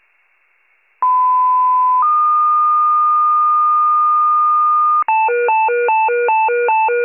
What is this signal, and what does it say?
Two-tone page, heard as a one-second tone, then a three-second tone, about 1010 then 1220 Hz. It is heard from 1.0 to 5.0 s.